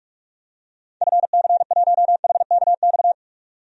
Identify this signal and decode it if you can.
Morse code, heard as a keyed tone, about 46 wpm, 700 Hz. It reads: FC1HKX
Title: DXCC-100 Call Signs - 6 Characters - 45wpm